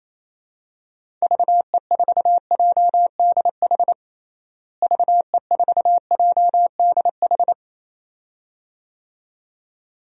Morse code, heard as a keyed tone, about 28 wpm, 700 Hz. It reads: VE4JDH VE4JDH
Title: DXCC-100 Call Signs - 6 Characters - Repeated 2x - 28wpm